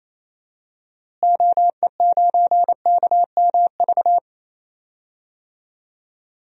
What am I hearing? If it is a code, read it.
Morse code, heard as a keyed tone, about 27 wpm, 700 Hz. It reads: OE9KMV
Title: DXCC-100 Call Signs - 6 Characters - Repeated 3x - 28wpm